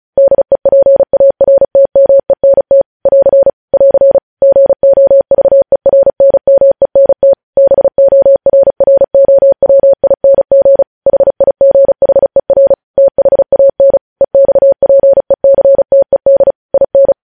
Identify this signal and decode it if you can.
Morse code, heard as a keyed tone, about 36 wpm, 550 Hz. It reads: DEPARTMENT <AR> <AR> GOVERNMENT BORROWING HIGHER THAN EXPECTED IN
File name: podcast / News headlines-1x-35x35